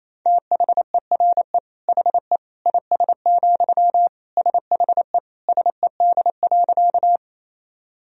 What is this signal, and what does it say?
Morse code, heard as a keyed tone, about 28 wpm, 700 Hz. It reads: THERE HE IS, SHE SED.
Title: Sentences from Top 100 Words and Abbreviations 28wpm